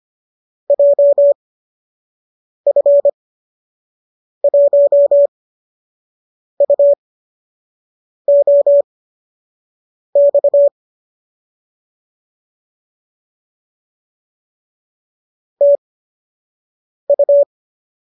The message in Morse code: JF1UOX TU

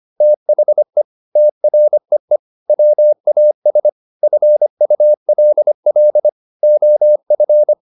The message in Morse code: THE TREE WAS FULL OF

